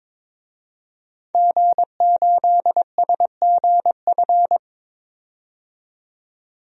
Morse code, heard as a keyed tone, about 21 wpm, 700 Hz. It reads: G8SGF
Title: DXCC-100 Call Signs - 5 Characters - 22wpm